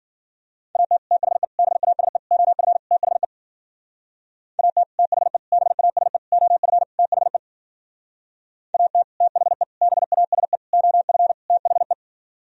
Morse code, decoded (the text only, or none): AT THE BASE OF THE AT THE BASE OF THE AT THE BASE OF THE